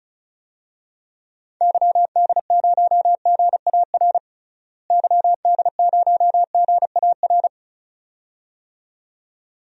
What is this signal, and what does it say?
Morse code, heard as a keyed tone, about 34 wpm, 700 Hz. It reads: YD0GAR YD0GAR
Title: DXCC-100 Call Signs - 6 Characters - Repeated 2x - 35wpm